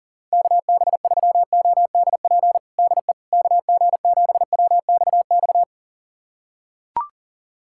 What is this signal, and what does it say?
Morse code, heard as a keyed tone, about 40 wpm, 700 Hz. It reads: KB3ODP DE KG7WXX
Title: USA Callsign DE Callsign 40wpm